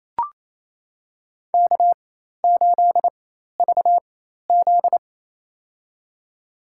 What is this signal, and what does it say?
Morse code, heard as a keyed tone, about 28 wpm, spaced as for 13 wpm, 700 Hz. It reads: K8VZ
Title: USA Call Signs - 4x Character Spacing - 28wpm